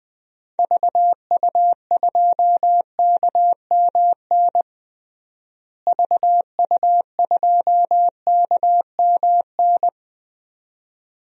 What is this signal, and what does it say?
Morse code, heard as a keyed tone, about 20 wpm, 700 Hz. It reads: VU2KMN VU2KMN
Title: DXCC-100 Call Signs - 6 Characters - Repeated 2x - 20wpm